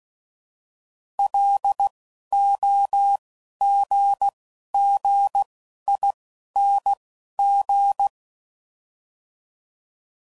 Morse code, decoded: L O G G I N G

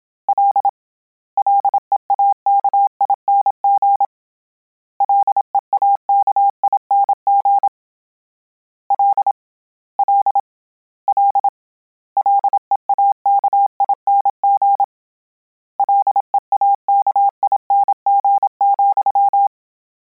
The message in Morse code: L LEAKING LEAKING L L L LEAKING LEAKING,

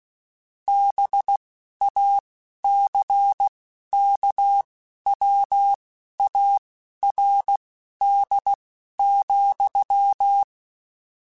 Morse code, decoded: BACKWARD,

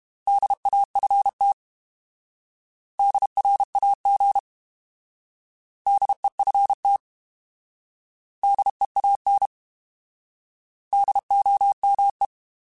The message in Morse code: DAFT DRAG DEFT DEAN DOME

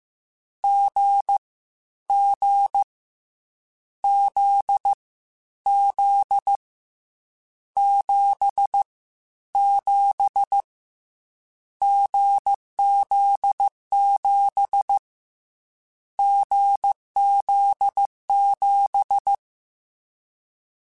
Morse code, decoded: G G Z Z 7 7 GZ7 GZ7

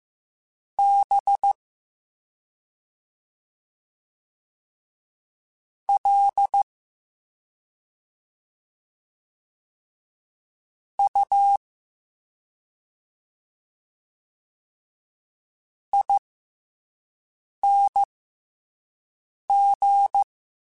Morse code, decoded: B L U ING

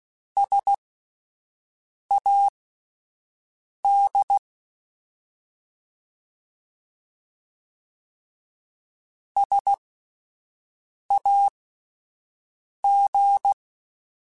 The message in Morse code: SAD SAG

